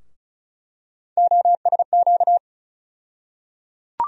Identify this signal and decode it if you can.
Morse code, heard as a keyed tone, about 34 wpm, 700 Hz. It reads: OSQ
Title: USA Call Sign Postfixes - 3 Letters 35wpm